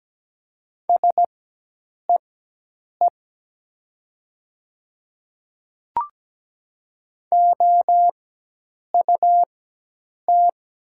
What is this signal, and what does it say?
Morse code, heard as a keyed tone, about 17 wpm, spaced as for 8 wpm, 700 Hz. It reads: SEE OUT
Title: 4x Character Spacing - Top 200 Words - 17wpm